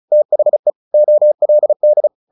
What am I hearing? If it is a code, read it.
Morse code, heard as a keyed tone, about 34 wpm, 600 Hz. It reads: THE OLD
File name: the old man and the sea / the old man and the sea 1 35wpm